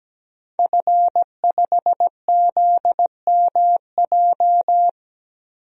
Morse code, heard as a keyed tone, about 17 wpm, 700 Hz. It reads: F5ZMJ
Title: DXCC-100 Call Signs - 5 Characters - Repeated 2x - 17wpm